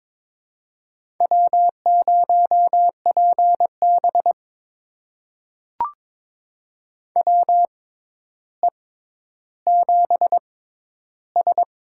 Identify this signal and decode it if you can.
Morse code, heard as a keyed tone, about 22 wpm, spaced as for 11 wpm, 700 Hz. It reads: W0PB WE7S
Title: USA Call Signs - 4 Characters - 6x Character Spacing - 22wpm